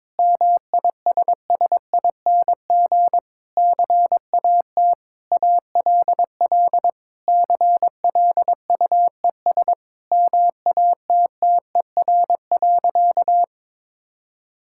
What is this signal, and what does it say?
Morse code, heard as a keyed tone, about 22 wpm, 700 Hz. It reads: MISSING CAT ALL CLUES MATTER.
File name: practice / 22wpm / Sentences with 4-Word Phrases - 22wpm